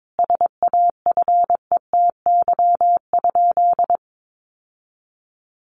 Morse code, heard as a keyed tone, about 22 wpm, 700 Hz. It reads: SAFETY?
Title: Sentences from Top 700 Words 22wpm